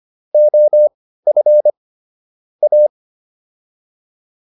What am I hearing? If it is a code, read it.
Morse code, heard as a keyed tone, about 25 wpm, spaced as for 15 wpm, 600 Hz. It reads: OF A